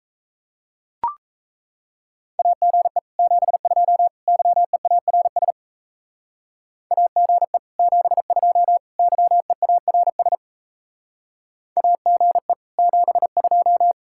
Morse code, heard as a keyed tone, about 34 wpm, 700 Hz. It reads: AGE 72 YEARS AGE 72 YEARS AGE 72
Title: Speed-Racing Common QSO Phrases - 28wpm